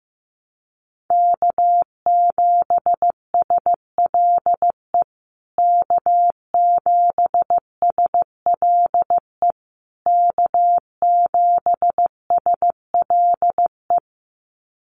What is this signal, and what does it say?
Morse code, heard as a keyed tone, about 15 wpm, 700 Hz. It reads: K7SLE K7SLE K7SLE
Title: USA Call Signs Repeated 3x 15wpm